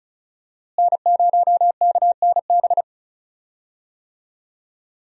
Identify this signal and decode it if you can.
Morse code, heard as a keyed tone, about 36 wpm, 700 Hz. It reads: N0KNB
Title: USA Call Signs - 5 Characters - Repeated 2x - 35wpm